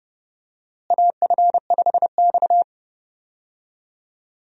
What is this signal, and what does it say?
Morse code, heard as a keyed tone, about 32 wpm, 700 Hz. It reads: AF5X
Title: USA Call Signs - 4 Characters - Repeated 3x - 30wpm